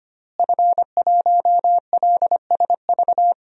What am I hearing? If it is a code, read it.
Morse code, heard as a keyed tone, about 25 wpm, 700 Hz. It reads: F1LSV
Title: DXCC-100 Call Signs - 5 Characters - Repeated 2x - 25wpm